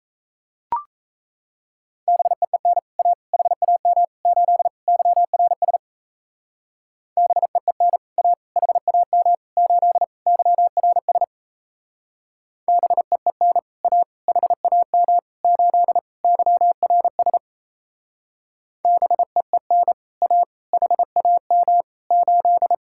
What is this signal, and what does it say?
Morse code, heard as a keyed tone, about 37 wpm, 700 Hz. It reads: BEEN A HAM 8 YRS BEEN A HAM 8 YRS BEEN A HAM 8 YRS BEEN A HAM 8